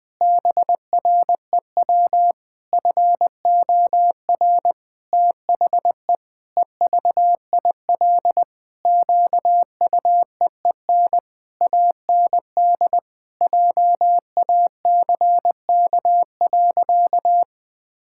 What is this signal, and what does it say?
Morse code, heard as a keyed tone, about 20 wpm, 700 Hz. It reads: BREW FOR THE EVIL QUEEN AND JACK.
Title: Pangram Challenge v2 - 20wpm